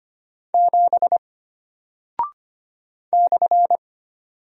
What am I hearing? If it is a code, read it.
Morse code, heard as a keyed tone, about 25 wpm, 700 Hz. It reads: 7 /
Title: Single Character (Letter:Number:Punctuation:Prosign) Alt 25wpm